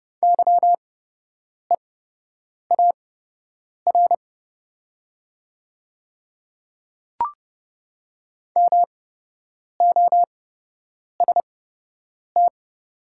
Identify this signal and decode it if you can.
Morse code, heard as a keyed tone, about 30 wpm, spaced as for 8 wpm, 700 Hz. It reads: YEAR MOST
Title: 8x Character Spacing - Top 300 Words - 30wpm